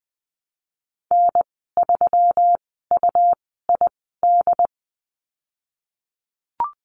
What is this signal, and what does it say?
Morse code, heard as a keyed tone, about 20 wpm, spaced as for 15 wpm, 700 Hz. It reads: N3UID
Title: 122 Lesson 20g - Number 3 - Call Signs 20wpm